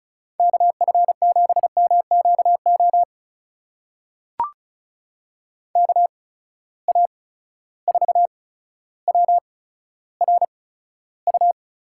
Morse code, heard as a keyed tone, about 36 wpm, spaced as for 15 wpm, 700 Hz. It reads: KF7MQO KA4WRU
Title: USA Call Signs - 6 Characters - 8x Character Spacing - 35wpm